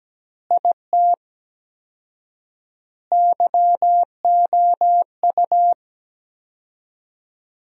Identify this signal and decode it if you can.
Morse code, heard as a keyed tone, about 17 wpm, 700 Hz. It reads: IT YOU
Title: 4x Word-Spacing - Sentences from Top 200 Words - 17wpm